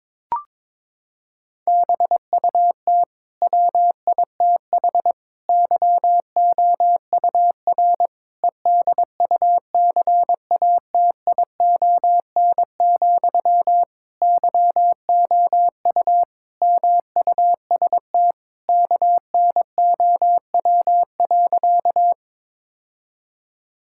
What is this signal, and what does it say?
Morse code, heard as a keyed tone, about 22 wpm, 700 Hz. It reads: BUT WITH YOUR EDUCATION, YOU MUST KNOW.
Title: Sentences from Top 1000 Words - Review Part 6 - 22wpm